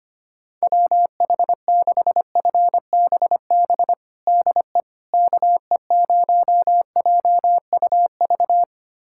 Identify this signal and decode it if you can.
Morse code, heard as a keyed tone, about 25 wpm, 700 Hz. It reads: WH6FBB DE KE0JUV
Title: USA Callsign DE Callsign 25wpm